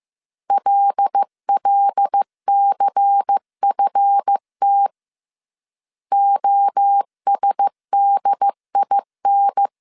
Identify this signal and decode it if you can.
Morse code, heard as a keyed tone, about 15 wpm, 800 Hz. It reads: LLCFT OSDIN
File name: CW / DET / 1 9 1500C S B 14wpm 3302